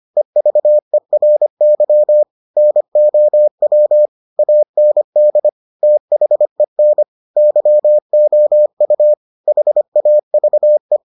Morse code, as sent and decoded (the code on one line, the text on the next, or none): . ...- . .-. -.-- / -. --- .-- / .- -. -.. / - .... -. / -.-- --- ..- / .... .- ...- .
EVERY NOW AND THEN YOU HAVE